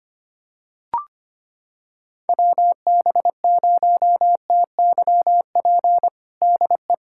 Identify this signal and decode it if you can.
Morse code, heard as a keyed tone, about 25 wpm, 700 Hz. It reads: WB0TYP DE
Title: USA Callsign DE Callsign - 6 Characters - 25wpm